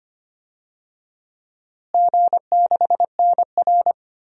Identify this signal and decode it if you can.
Morse code, heard as a keyed tone, about 25 wpm, 700 Hz. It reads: G6NR
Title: DXCC-100 Call Signs - 4 Characters - Repeated 3x - 25wpm